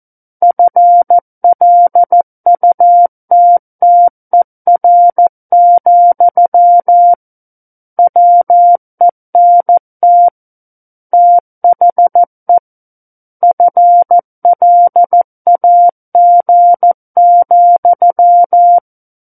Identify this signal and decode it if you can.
Morse code, heard as a keyed tone, about 14 wpm, 700 Hz. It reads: FLUTTER, WENT THE FLAG,
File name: Book One / 05-Chapter 5 - 14 WPM